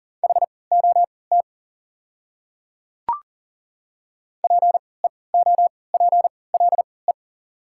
Morse code, heard as a keyed tone, about 40 wpm, spaced as for 23 wpm, 700 Hz. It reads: HOT PEOPLE
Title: 3x Character Spacing - Top 300 Words - 40wpm